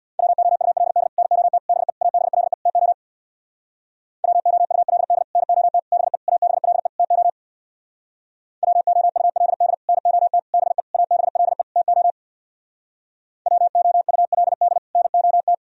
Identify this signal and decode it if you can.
Morse code, heard as a keyed tone, about 60 wpm, 700 Hz. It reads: WOULD NOT BE ABLE TO WOULD NOT BE ABLE TO WOULD NOT BE ABLE TO WOULD NOT
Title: Speed-Racing - Sets of 5 Words 50wpm